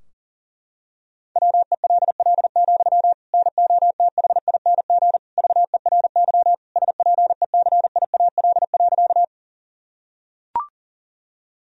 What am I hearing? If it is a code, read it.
Morse code, heard as a keyed tone, about 40 wpm, 700 Hz. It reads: WELL, NOTHING VERY SPECIAL.
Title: Sentences from Top 1000 Words - Review Part 3 - 40wpm